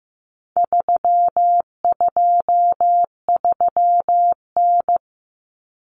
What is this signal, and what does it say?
Morse code, heard as a keyed tone, about 15 wpm, 700 Hz. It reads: 323N
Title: Common Cut Numbers - 4-Digit 15wpm